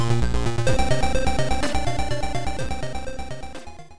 There is no code.